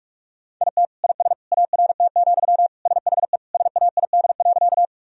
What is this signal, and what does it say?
Morse code, heard as a keyed tone, about 46 wpm, 700 Hz. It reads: IT IS ART, SHE SAID.
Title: Sentences from Top 1000 Words - Review Part 3 - 45wpm